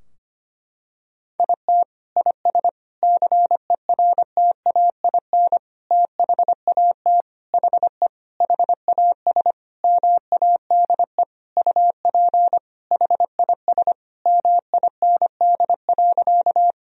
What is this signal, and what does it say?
Morse code, heard as a keyed tone, about 25 wpm, 700 Hz. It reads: IT IS CERTAIN THAT HE HAS MADE UP HIS MIND.
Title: Sentences from Top 1000 Words - Review Part 2 - 25wpm